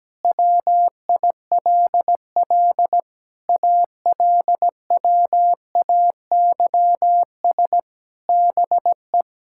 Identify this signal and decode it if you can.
Morse code, heard as a keyed tone, about 17 wpm, 700 Hz. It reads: WILL ALWAYS BE
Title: Sentences from Top 1600 Words 17wpm